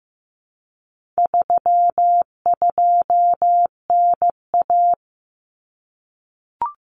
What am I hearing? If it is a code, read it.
Morse code, heard as a keyed tone, about 15 wpm, 700 Hz. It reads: 32NA
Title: Common Cut Numbers - 4-Digit 15wpm